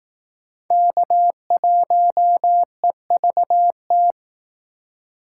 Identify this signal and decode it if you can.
Morse code, heard as a keyed tone, about 19 wpm, 700 Hz. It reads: K1EVT